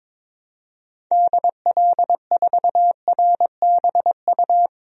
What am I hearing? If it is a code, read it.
Morse code, heard as a keyed tone, about 22 wpm, 700 Hz. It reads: DL4RBU
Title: DXCC-100 Call Signs - 6 Characters - Repeated 2x - 22wpm